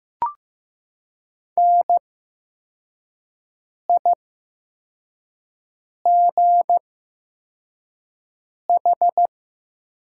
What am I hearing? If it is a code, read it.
Morse code, heard as a keyed tone, about 15 wpm, spaced as for 4 wpm, 700 Hz. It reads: NIGH
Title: 8x Character Spacing - Top 300 Words - 15wpm